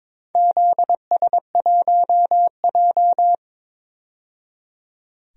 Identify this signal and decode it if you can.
Morse code, heard as a keyed tone, about 22 wpm, 700 Hz. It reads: ZS1J
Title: DXCC-100 Call Signs - 4 Characters - 22wpm